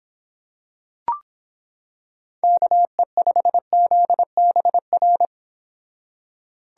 Morse code, heard as a keyed tone, about 26 wpm, 700 Hz. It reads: KE5ZBR